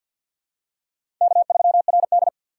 Morse code, heard as a keyed tone, about 52 wpm, 690 Hz. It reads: K3RD
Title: USA Call Signs - 6x Character Spacing - 50wpm